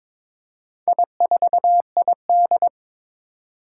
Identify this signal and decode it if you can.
Morse code, heard as a keyed tone, about 22 wpm, 700 Hz. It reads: I4ID
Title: Four Challenging Letter-Number Combinations 22wpm